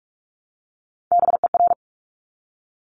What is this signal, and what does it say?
Morse code, heard as a keyed tone, about 46 wpm, 700 Hz. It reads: BER
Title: Top Three Letter Combinations in Words 45wpm